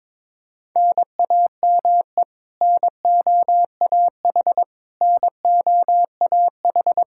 Morse code, heard as a keyed tone, about 22 wpm, 700 Hz. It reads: NAME NOAH NOAH
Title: Practice QSO - Exchange 1 - Beginning - 22wpm